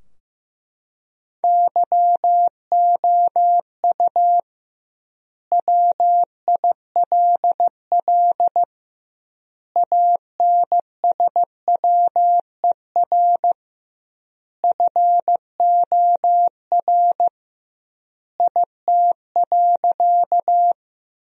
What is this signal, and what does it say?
Morse code, heard as a keyed tone, about 16 wpm, 700 Hz. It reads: YOU WILL ANSWER FOR IT.
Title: 2x Word-Spacing - Sentences from Top 300 Words - 15wpm